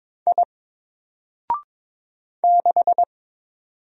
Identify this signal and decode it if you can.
Morse code, heard as a keyed tone, about 22 wpm, 700 Hz. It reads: I 6
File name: Single-Letter-Number-22wpm